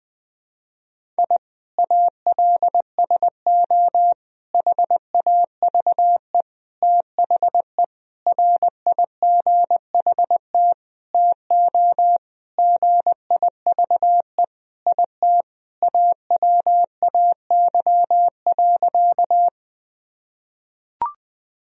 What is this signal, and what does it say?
Morse code, heard as a keyed tone, about 20 wpm, 700 Hz. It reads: I ALSO HAVE THE RIGHT TO GIVE IT AWAY.
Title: Sentences from Top 1000 Words - Review Part 1 - 20wpm